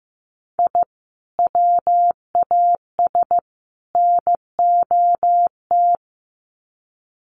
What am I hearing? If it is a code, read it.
Morse code, heard as a keyed tone, about 15 wpm, 700 Hz. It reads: I WAS NOT
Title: Sets of 3 Words 15wpm